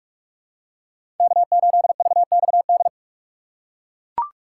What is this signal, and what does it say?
Morse code, heard as a keyed tone, about 42 wpm, 700 Hz. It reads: K8VXD
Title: USA Call Signs - 5 Characters - 6x Character Spacing - 45wpm